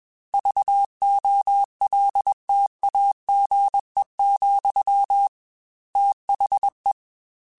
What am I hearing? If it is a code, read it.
Morse code, heard as a keyed tone, about 21 wpm, 800 Hz. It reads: VOLTAGE, THE
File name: Technical Subjects / Technical Subjects-mp3s / XC @20